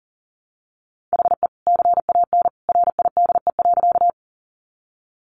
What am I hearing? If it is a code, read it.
Morse code, heard as a keyed tone, about 40 wpm, 700 Hz. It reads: HE CAN RIDE.
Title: Sentences from Top 1200 Words 40wpm